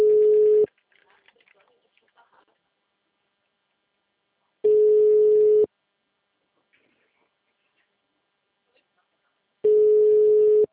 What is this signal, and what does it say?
Ringback tone (Europe, 425 Hz), with speech.